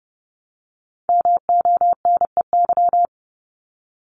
Morse code, heard as a keyed tone, about 30 wpm, 700 Hz. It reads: MONEY
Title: Multi-letter Phonemes Repeated 3x 30wpm